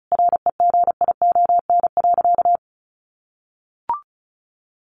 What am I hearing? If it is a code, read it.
Morse code, heard as a keyed tone, about 34 wpm, 700 Hz. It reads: REGION.
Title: Sentences from Top 1500 Words 35wpm